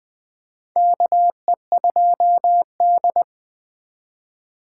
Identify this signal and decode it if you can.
Morse code, heard as a keyed tone, about 21 wpm, 700 Hz. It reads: KE2D
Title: K1USN SST QSO Practice - 20wpm